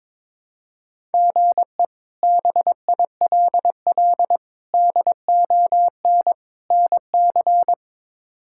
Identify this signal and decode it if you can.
Morse code, heard as a keyed tone, about 22 wpm, 700 Hz. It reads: GE BILL DON NC
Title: K1USN SST QSO Practice - 22wpm